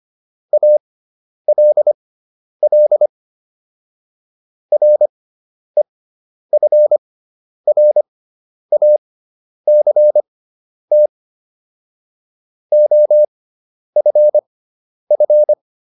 Morse code, heard as a keyed tone, about 25 wpm, spaced as for 10 wpm, 600 Hz. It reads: ALL REFRACT OFF